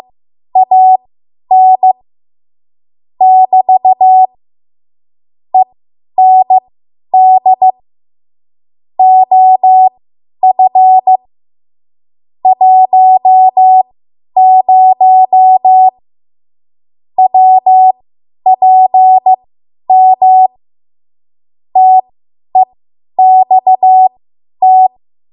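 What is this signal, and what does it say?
Morse code, heard as a keyed tone, about 15 wpm, spaced as for 10 wpm, 750 Hz. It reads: AN <BT> END OF 10 WPM TEXT
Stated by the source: Morse code practice run at 10 WPM from a code practice archive.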